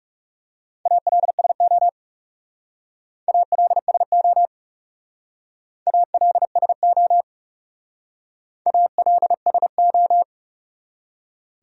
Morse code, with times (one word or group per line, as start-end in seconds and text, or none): 0.9-1.9 ALSO
3.3-4.5 ALSO
5.9-7.2 ALSO
8.7-10.2 ALSO